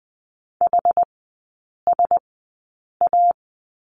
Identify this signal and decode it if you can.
Morse code, heard as a keyed tone, about 20 wpm, spaced as for 8 wpm, 700 Hz. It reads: HSA
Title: 127 Lesson 21e - Number 6 - Three Characters 20wpm